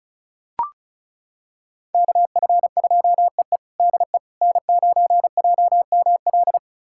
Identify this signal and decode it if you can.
Morse code, heard as a keyed tone, about 34 wpm, 700 Hz. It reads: KF2EE DE N9JML
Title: USA Callsign DE Callsign - 5 Characters - 35wpm